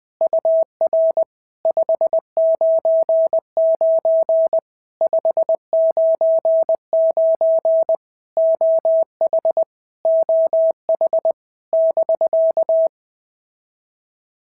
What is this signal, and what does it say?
Morse code, heard as a keyed tone, about 20 wpm, 650 Hz. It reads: UR 599 599 OH OH <BK>
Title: POTA QSO Practice - 20wpm